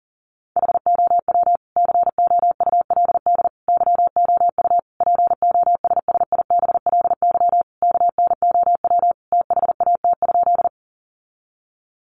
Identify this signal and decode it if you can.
Morse code, heard as a keyed tone, about 40 wpm, 700 Hz. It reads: HOW COULD YOU POSSIBLY KNOW THAT?
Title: Sentences from Top 1300 Words 40wpm